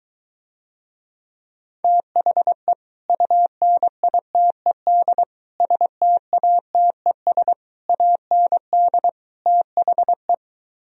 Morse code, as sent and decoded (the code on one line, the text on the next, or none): - .... / ..- -. .. - . -.. / ... - .- - . ... / .- -. -.. / - .... .
THE UNITED STATES AND THE